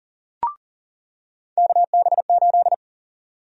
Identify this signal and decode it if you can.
Morse code, heard as a keyed tone, about 40 wpm, 700 Hz. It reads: KB8
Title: USA Call Sign Prefixes - 40wpm